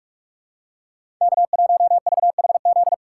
Morse code, heard as a keyed tone, about 44 wpm, 700 Hz. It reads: K1VHZ
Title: USA Call Signs - 5 Characters - 8x Character Spacing - 45wpm